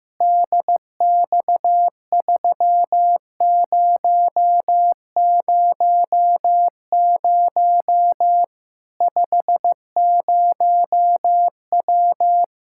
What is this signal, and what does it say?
Morse code, heard as a keyed tone, about 15 wpm, 700 Hz. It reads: DX3000 50W
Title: Practice QSO - Exchange 2 - Info - 15wpm